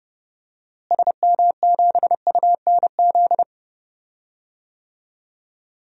Morse code, heard as a keyed tone, about 30 wpm, 700 Hz. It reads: SM7UNZ
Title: DXCC-100 Call Signs - 6 Characters - Repeated 2x - 30wpm